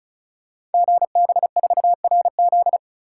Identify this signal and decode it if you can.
Morse code, heard as a keyed tone, about 36 wpm, 700 Hz. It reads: GB4RZ
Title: Code Groups - 5 Letter:Numbers - 35wpm